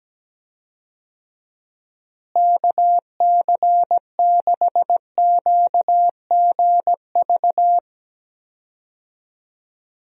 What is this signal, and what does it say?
Morse code, heard as a keyed tone, about 17 wpm, 700 Hz. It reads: KC6QGV
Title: USA Call Signs - 6 Characters - Repeated 3x - 17wpm